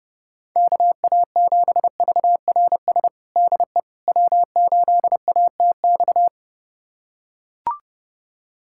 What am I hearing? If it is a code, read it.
Morse code, heard as a keyed tone, about 32 wpm, 700 Hz. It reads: KA7VRS DE W8ATX
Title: USA Callsign DE Callsign 30wpm